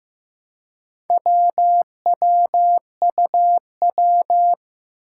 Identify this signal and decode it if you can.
Morse code, heard as a keyed tone, about 15 wpm, 700 Hz. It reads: WWUW
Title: Four Challenging Letter-Number Combinations 15wpm